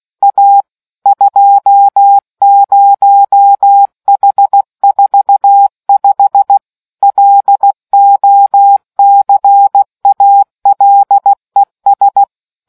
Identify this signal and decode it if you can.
Morse code, heard as a keyed tone, about 16 wpm, 800 Hz. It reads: A 20H45 LOCALES